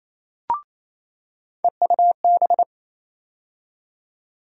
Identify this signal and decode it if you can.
Morse code, heard as a keyed tone, about 28 wpm, 700 Hz. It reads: EUB
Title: USA Call Sign Postfixes - 3 Letters 28wpm